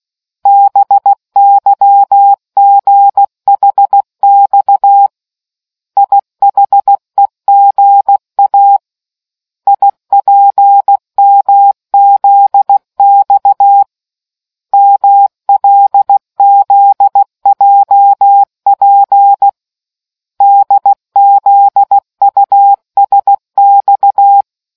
Morse code, heard as a keyed tone, about 16 wpm, 800 Hz. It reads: BYGHX IHEGA IPMZX MLZJP DZUSX